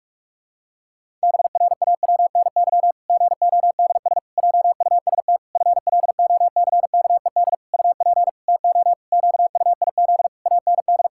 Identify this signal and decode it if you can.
Morse code, heard as a keyed tone, about 46 wpm, 700 Hz. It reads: BRAWNY GODS JUST FLOCKED UP TO QUIZ AND
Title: Pangram Challenge v2 - 45wpm